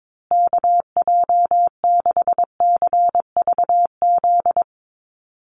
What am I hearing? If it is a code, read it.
Morse code, heard as a keyed tone, about 22 wpm, 700 Hz. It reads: KJ6CVZ
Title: USA Call Signs Repeated 2x 22wpm